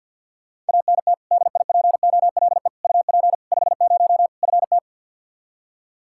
Morse code, heard as a keyed tone, about 52 wpm, 700 Hz. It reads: ANT DIPOLE UP 50 FT
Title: Practice QSO - Exchange 2 - Info - 50wpm